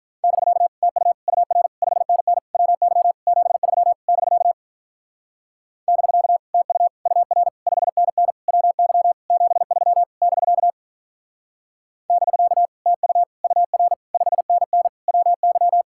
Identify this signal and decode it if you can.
Morse code, heard as a keyed tone, about 44 wpm, 700 Hz. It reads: <BK> TU UR 5NN WY 73 <BK> <BK> TU UR 5NN WY 73 <BK> <BK> TU UR 5NN WY